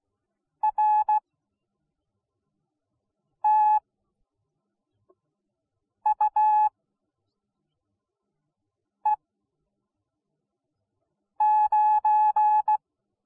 Morse code, heard as a keyed tone, about 16 wpm, spaced as for 4 wpm, 850 Hz. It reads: RTUE9